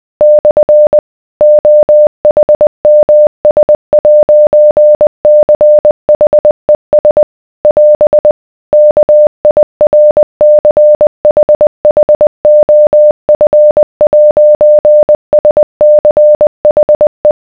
Morse code, heard as a keyed tone, about 20 wpm, 600 Hz. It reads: / OHMS'CHES <AS> KIRCHHOF'SCHE